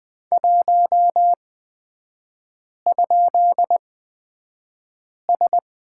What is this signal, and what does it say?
Morse code, heard as a keyed tone, about 20 wpm, spaced as for 5 wpm, 700 Hz. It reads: 1?S
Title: 255 Lesson 41e - Period - Three Characters 20wpm